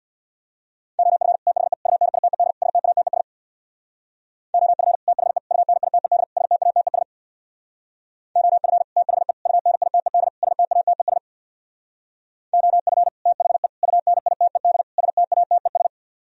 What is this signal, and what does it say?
Morse code, heard as a keyed tone, about 58 wpm, 700 Hz. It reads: OF THE UNITED STATES OF THE UNITED STATES OF THE UNITED STATES OF THE UNITED STATES
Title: Speed-Racing - Sets of 4 Words 50wpm